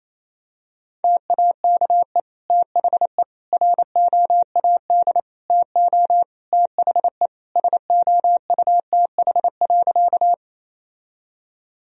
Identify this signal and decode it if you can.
Morse code, heard as a keyed tone, about 28 wpm, 700 Hz. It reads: TAKE THE ROAD TO THE SOUTH.
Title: Sentences from Top 1000 Words - Review Part 2 - 28wpm